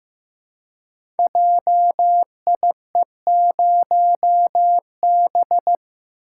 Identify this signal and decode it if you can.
Morse code, heard as a keyed tone, about 15 wpm, 700 Hz. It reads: JIE0B
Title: Code Groups - 5 Letter:Numbers - 15wpm